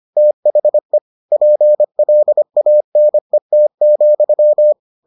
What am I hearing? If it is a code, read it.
Morse code, heard as a keyed tone, about 25 wpm, 600 Hz. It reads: THE PLANET,